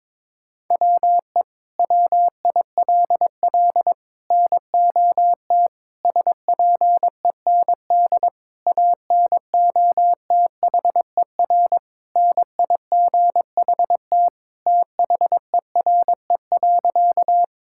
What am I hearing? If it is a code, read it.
Morse code, heard as a keyed tone, about 22 wpm, 700 Hz. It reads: WE WILL NOT SPEND ANOTHER NIGHT THERE.
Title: Sentences from Top 1000 Words - Review Part 5 - 22wpm